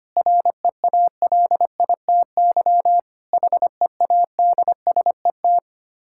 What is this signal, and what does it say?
Morse code, heard as a keyed tone, about 25 wpm, 700 Hz. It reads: REALITY HEADSET
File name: practice / 25wpm / Difficult Words with Hints - 25wpm